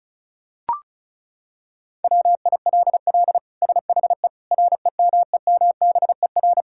Morse code, heard as a keyed tone, about 36 wpm, 700 Hz. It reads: WILL SHE REMEMBER
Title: Sentences from Top 1000 Words - Review Part 6 - 35wpm